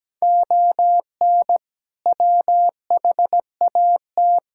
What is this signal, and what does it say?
Morse code, heard as a keyed tone, about 17 wpm, 700 Hz. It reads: ON WHAT
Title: Sentences from Top 1000 Words - Review Part 5 - 17wpm